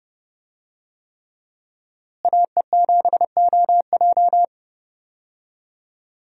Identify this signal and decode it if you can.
Morse code, heard as a keyed tone, about 30 wpm, 700 Hz. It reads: AE7OJ
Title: USA Call Signs - 5 Characters - Repeated 3x - 30wpm